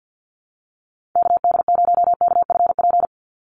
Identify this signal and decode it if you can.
Morse code, heard as a keyed tone, about 49 wpm, 700 Hz. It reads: KD0KFP
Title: USA Call Signs 50wpm